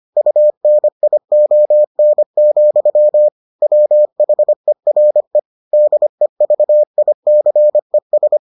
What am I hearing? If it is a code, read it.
Morse code, heard as a keyed tone, about 25 wpm, 600 Hz. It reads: UNION, WHERE DEVICES